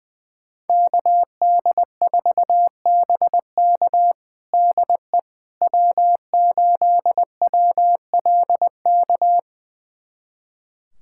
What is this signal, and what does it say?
Morse code, heard as a keyed tone, about 21 wpm, 700 Hz. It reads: KD4BK DE W8WLK
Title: USA Callsign DE Callsign 20wpm